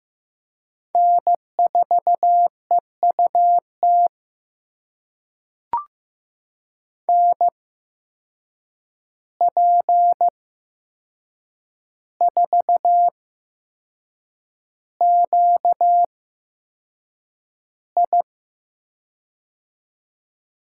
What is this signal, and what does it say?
Morse code, heard as a keyed tone, about 15 wpm, spaced as for 6 wpm, 700 Hz. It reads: N4EUT NP4QI
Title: USA Call Signs - 5 Characters - 8x Character Spacing - 15wpm